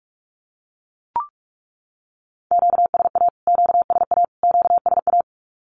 Morse code, heard as a keyed tone, about 44 wpm, 700 Hz. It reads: QSA QSA QSA
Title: Q-Codes Repeated 3x 45wpm